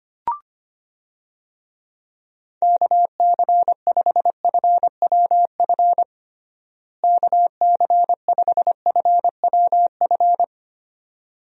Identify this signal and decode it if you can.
Morse code, heard as a keyed tone, about 25 wpm, 700 Hz. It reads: KC5FWF KC5FWF
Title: USA Call Signs - 6 Characters - Repeated 2x - 25wpm